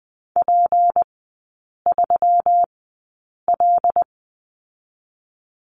Morse code, heard as a keyed tone, about 20 wpm, spaced as for 8 wpm, 700 Hz. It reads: P3L
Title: 152 Lesson 25e - Letter p - Three Characters 20wpm